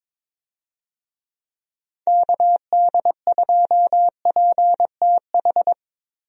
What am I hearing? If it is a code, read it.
Morse code, heard as a keyed tone, about 22 wpm, 700 Hz. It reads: KD2PTH
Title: USA Call Signs - 6 Characters - Repeated 3x - 22wpm